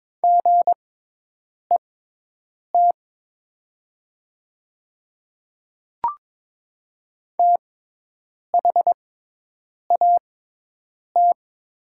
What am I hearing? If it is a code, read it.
Morse code, heard as a keyed tone, about 22 wpm, spaced as for 8 wpm, 700 Hz. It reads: GET THAT
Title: 6x Character Spacing - Top 300 Words - 22wpm